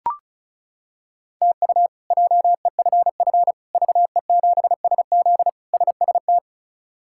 Morse code, heard as a keyed tone, about 34 wpm, 700 Hz. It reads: TU JEFF VE7SZ SST
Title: K1USN SST QSO Practice - 35wpm